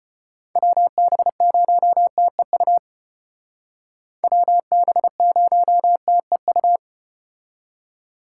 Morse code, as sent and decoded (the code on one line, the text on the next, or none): .-- -... ----- - . ..- / .-- -... ----- - . ..-
WB0TEU WB0TEU